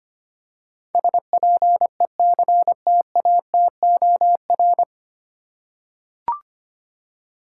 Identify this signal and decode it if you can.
Morse code, heard as a keyed tone, about 25 wpm, 700 Hz. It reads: SPECTATOR
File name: practice / 25wpm / 9 Letter Words - First 3 Letters Given - 25wpm